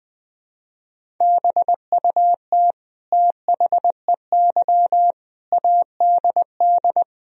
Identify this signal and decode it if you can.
Morse code, heard as a keyed tone, about 20 wpm, 700 Hz. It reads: BUT THEY ADD
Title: Sentences from Top 1000 Words - Review Part 1 - 20wpm